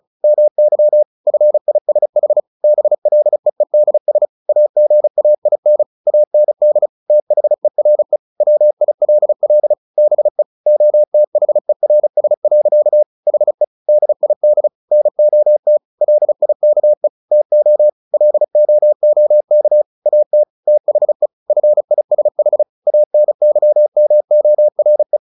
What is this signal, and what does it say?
Morse code, heard as a keyed tone, about 34 wpm, 600 Hz. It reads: MY FISH BLEEDS AGAIN AND THERE WILL BE OTHERS. HE DID NOT LIKE TO LOOK AT THE FISH ANYMORE
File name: the old man and the sea 2 35wpm